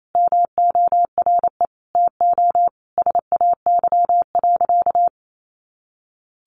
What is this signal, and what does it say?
Morse code, heard as a keyed tone, about 28 wpm, 700 Hz. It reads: MORE TO SAY.
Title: Sentences from Top 500 Words 28wpm